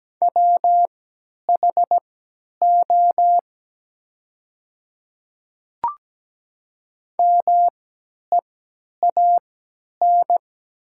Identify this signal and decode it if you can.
Morse code, heard as a keyed tone, about 17 wpm, spaced as for 10 wpm, 700 Hz. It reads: WHO MEAN
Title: 3x Character Spacing - Top 200 Words - 17wpm